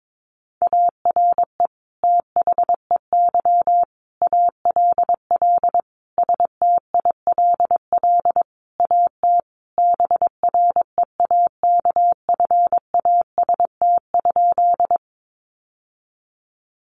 Morse code, heard as a keyed tone, about 22 wpm, 700 Hz. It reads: ARE THEY ALL STILL AT BREAKFAST?
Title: Sentences from Top 1500 Words 22wpm